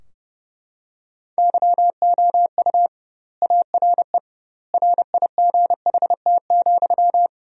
Morse code, heard as a keyed tone, about 30 wpm, 700 Hz. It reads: YOU ARE RIGHT,
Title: 2x Word-Spacing - Sentences from Top 400 Words - 30wpm